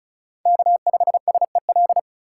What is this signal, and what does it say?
Morse code, heard as a keyed tone, about 36 wpm, 700 Hz. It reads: K5SEL
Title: USA Call Signs - 5 Characters - 4x Character Spacing - 35wpm